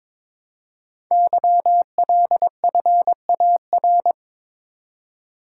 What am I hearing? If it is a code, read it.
Morse code, heard as a keyed tone, about 22 wpm, 700 Hz. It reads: YLFAR
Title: Code Groups - 5 Letters - 22wpm